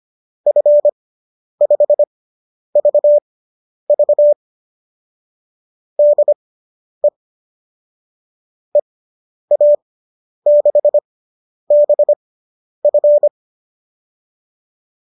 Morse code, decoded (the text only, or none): F5VV DE EA6BF